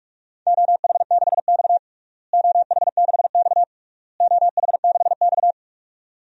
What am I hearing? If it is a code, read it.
Morse code, heard as a keyed tone, about 46 wpm, 690 Hz. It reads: OH6X OH6X OH6X
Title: DXCC-100 Call Signs - 4 Characters - Repeated 3x - 45wpm